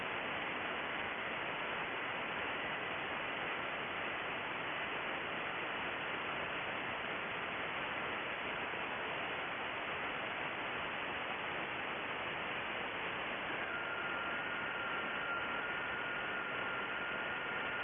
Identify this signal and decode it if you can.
FT8.
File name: contests / sept-vhf-k5tr- -2200